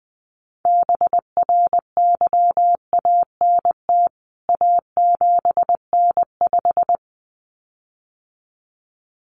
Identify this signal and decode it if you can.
Morse code, heard as a keyed tone, about 20 wpm, 700 Hz. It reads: BRYANT A7N5
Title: CWT Exchange 20wpm